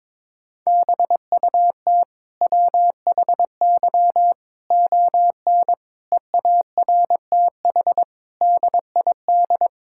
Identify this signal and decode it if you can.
Morse code, heard as a keyed tone, about 22 wpm, 700 Hz. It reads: BUT WHY ON EARTH DID
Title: Sentences from Top 1000 Words - Review Part 6 - 22wpm